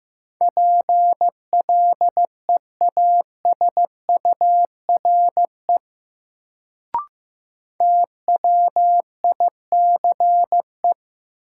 Morse code, heard as a keyed tone, about 15 wpm, 700 Hz. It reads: PLEASURE TWICE
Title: Sentences from Top 1600 Words - New Word Drill - 15wpm